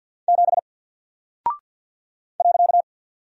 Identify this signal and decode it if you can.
Morse code, heard as a keyed tone, about 48 wpm, 700 Hz. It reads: 7 .
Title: Single Character (Letter:Number:Punctuation:Prosign) Alt 50wpm